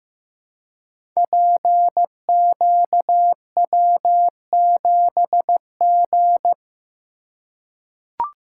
Morse code, heard as a keyed tone, about 15 wpm, 700 Hz. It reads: PQW7G
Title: Code Groups - 5 Letter:Numbers - 15wpm